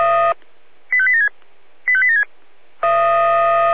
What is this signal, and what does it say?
Five-tone selective call (ZVEI), heard as quick runs of five tones, tones of about 70 ms. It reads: 85876 85876
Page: SDR/ZVEI